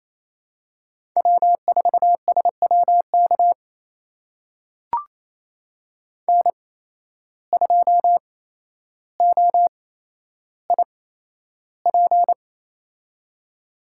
Morse code, heard as a keyed tone, about 28 wpm, spaced as for 12 wpm, 700 Hz. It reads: W4SWK N2OIP
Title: USA Call Signs - 5 Characters - 8x Character Spacing - 28wpm